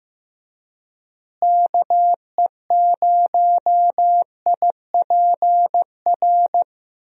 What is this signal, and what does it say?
Morse code, heard as a keyed tone, about 15 wpm, 700 Hz. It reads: KE0IPR
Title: USA Call Signs - 6 Characters - Repeated 3x - 15wpm